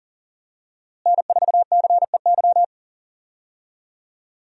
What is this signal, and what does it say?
Morse code, heard as a keyed tone, about 37 wpm, 700 Hz. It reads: N4CEY
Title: USA Call Signs - 5 Characters - Repeated 2x - 40wpm